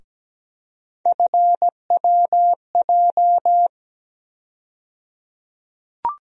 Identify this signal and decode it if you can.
Morse code, heard as a keyed tone, about 17 wpm, 700 Hz. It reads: FWJ
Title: USA Call Sign Postfixes - 3 Letters 17wpm